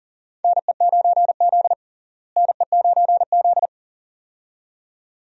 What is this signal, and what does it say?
Morse code, heard as a keyed tone, about 40 wpm, 700 Hz. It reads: NE9Z NE9Z
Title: USA Call Signs - 4 Characters - Repeated 2x - 40wpm